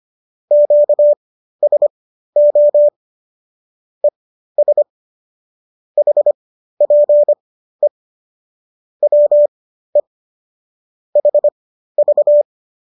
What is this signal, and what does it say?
Morse code, heard as a keyed tone, about 25 wpm, spaced as for 13 wpm, 590 Hz. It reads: QSO ES HPE WE HV